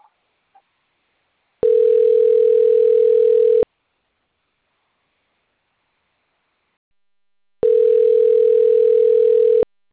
Ringback tone (North America), 440 + 480 Hz.